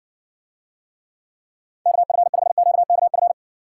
Morse code, heard as a keyed tone, about 60 wpm, 700 Hz. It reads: KF5QKF